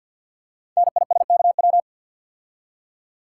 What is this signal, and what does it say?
Morse code, heard as a keyed tone, about 52 wpm, 700 Hz. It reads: NISKW
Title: Code Groups - 5 Letters - 50wpm